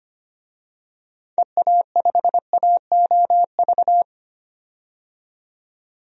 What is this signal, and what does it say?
Morse code, heard as a keyed tone, about 25 wpm, 700 Hz. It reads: EA5AOV
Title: DXCC-100 Call Signs - 6 Characters - 25wpm